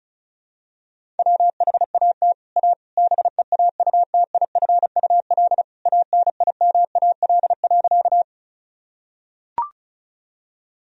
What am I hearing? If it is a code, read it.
Morse code, heard as a keyed tone, about 36 wpm, 700 Hz. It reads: WHAT A BEAUTIFUL ANIMAL.
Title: Sentences from Top 1000 Words - Review Part 1 - 35wpm